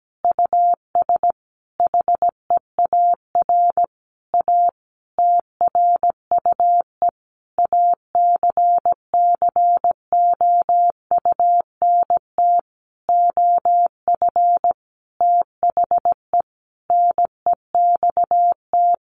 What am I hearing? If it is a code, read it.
Morse code, heard as a keyed tone, about 17 wpm, 700 Hz. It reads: US HEAR A TRUE ACCOUNT OF THE NEXT
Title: Sentences from Top 900 Words 17wpm